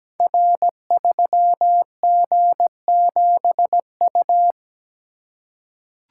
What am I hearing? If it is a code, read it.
Morse code, heard as a keyed tone, about 17 wpm, 700 Hz. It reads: R3G7U
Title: Code Groups - 5 Letter:Numbers - 17wpm